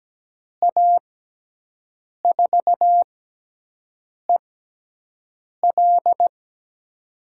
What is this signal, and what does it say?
Morse code, heard as a keyed tone, about 17 wpm, spaced as for 6 wpm, 700 Hz. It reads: A4EL
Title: USA Call Signs - 5 Characters - 6x Character Spacing - 17wpm